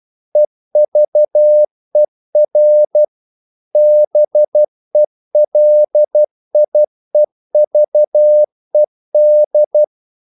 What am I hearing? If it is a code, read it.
Morse code, heard as a keyed tone, about 12 wpm, 600 Hz. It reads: EVER BELIEVED
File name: CW / Practice / EnglishQuotations0112